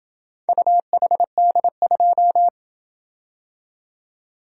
Morse code, heard as a keyed tone, about 26 wpm, 700 Hz. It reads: UHD2